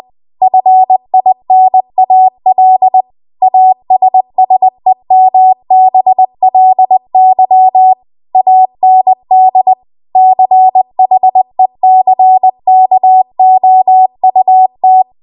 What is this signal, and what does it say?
Morse code, heard as a keyed tone, about 20 wpm, 750 Hz. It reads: FINAL ASSEMBLY AND CHECKOUT